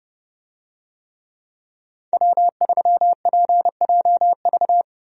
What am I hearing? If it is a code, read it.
Morse code, heard as a keyed tone, about 32 wpm, 700 Hz. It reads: W3PJV
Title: USA Call Signs - 5 Characters - Repeated 3x - 30wpm